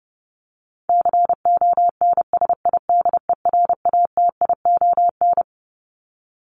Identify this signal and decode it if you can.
Morse code, heard as a keyed tone, about 30 wpm, 700 Hz. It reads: CONSIDERATION
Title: Top 1251-1500 Words 30wpm